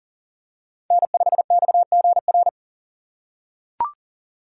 Morse code, heard as a keyed tone, about 37 wpm, 700 Hz. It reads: N5XGR
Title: USA Call Signs - 5 Characters - 3x Character Spacing - 40wpm